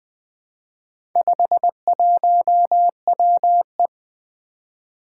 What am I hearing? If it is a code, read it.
Morse code, heard as a keyed tone, about 20 wpm, 700 Hz. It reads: 51WE
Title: Four Challenging Letter-Number Combinations 20wpm